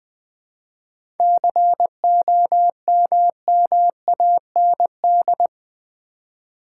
Morse code, heard as a keyed tone, about 20 wpm, 690 Hz. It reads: COMMAND
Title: 7 Letter Words - First 3 Letters Given - 20wpm